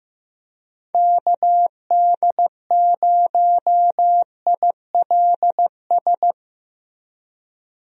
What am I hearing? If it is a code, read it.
Morse code, heard as a keyed tone, about 16 wpm, 700 Hz. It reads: KD0ILS
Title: USA Call Signs - 6 Characters - Repeated 3x - 15wpm